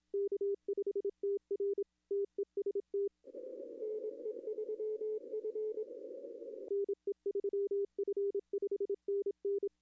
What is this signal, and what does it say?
Morse code, heard as a keyed tone, about 26 wpm, 410 Hz. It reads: K5TR TEST NE3F NE3F 5NN
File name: contests / arrl-160-k5tr-op-w5tn -0800